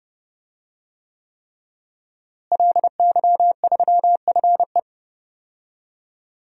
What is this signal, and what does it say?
Morse code, heard as a keyed tone, about 31 wpm, 700 Hz. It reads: LY3FE
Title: DXCC-100 Call Signs - 5 Characters - Repeated 3x - 30wpm